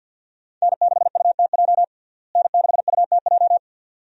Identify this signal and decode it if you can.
Morse code, heard as a keyed tone, about 53 wpm, 700 Hz. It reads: N6UTJ N6UTJ
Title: USA Call Signs - 5 Characters - Repeated 2x - 50wpm